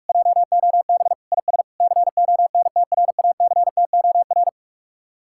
Morse code, heard as a keyed tone, about 44 wpm, 700 Hz. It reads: JOB IS CONTRACTOR